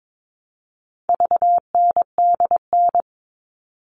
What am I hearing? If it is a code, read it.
Morse code, heard as a keyed tone, about 22 wpm, 700 Hz. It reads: VNDN
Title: All Cut Numbers - 4-Digit 22wpm